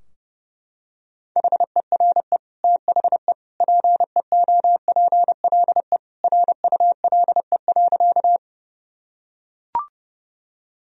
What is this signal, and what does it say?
Morse code, heard as a keyed tone, about 30 wpm, 700 Hz. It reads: HERE THE PEOPLE RULE.
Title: Sentences from Top 1000 Words - Review Part 2 - 30wpm